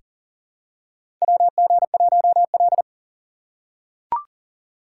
Morse code, heard as a keyed tone, about 38 wpm, 700 Hz. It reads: WG1L
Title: USA Call Signs - 6x Character Spacing - 40wpm